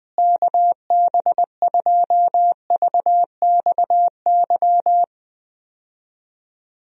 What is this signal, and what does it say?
Morse code, heard as a keyed tone, about 20 wpm, 700 Hz. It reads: KB2VXY
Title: USA Call Signs - 6 Characters - Repeated 3x - 20wpm